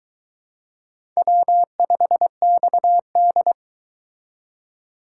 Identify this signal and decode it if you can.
Morse code, heard as a keyed tone, about 23 wpm, 700 Hz. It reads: W5XD